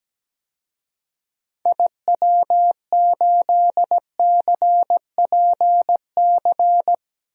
Morse code, heard as a keyed tone, about 17 wpm, 700 Hz. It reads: IW8CPC
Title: DXCC-100 Call Signs - 6 Characters - 17wpm